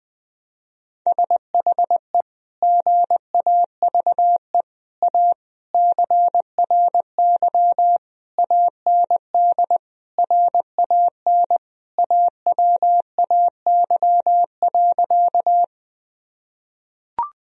Morse code, heard as a keyed tone, about 21 wpm, 700 Hz. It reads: SHE GAVE A CRY AND RAN AWAY.
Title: Sentences from Top 1000 Words - Review Part 3 - 20wpm